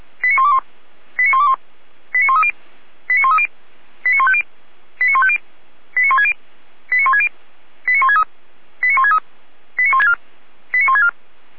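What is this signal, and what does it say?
Five-tone selective call (ZVEI), heard as quick runs of five tones, tones of about 80 ms. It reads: 89121 89121 89130 89130 89150 89150 89160 89160 89163 89163 89164 89164
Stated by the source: ZVEI five-tone selective call 89140 received by SDR, Feuerwehr Orlamünde fire brigade alerting.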